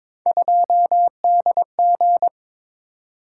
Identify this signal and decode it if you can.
Morse code, heard as a keyed tone, about 22 wpm, 700 Hz. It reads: 2DG
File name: practice / 22wpm / Three Letter-Number Combinations - 22wpm